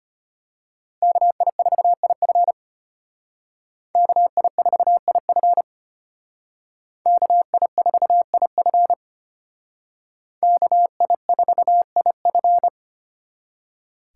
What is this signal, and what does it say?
Morse code, heard as a keyed tone, about 32 wpm, 700 Hz. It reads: KI4IF KI4IF KI4IF KI4IF